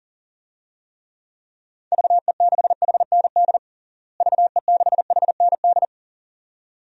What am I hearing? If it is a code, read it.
Morse code, heard as a keyed tone, about 40 wpm, 700 Hz. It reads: VE6HND VE6HND
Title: DXCC-100 Call Signs - 6 Characters - Repeated 2x - 40wpm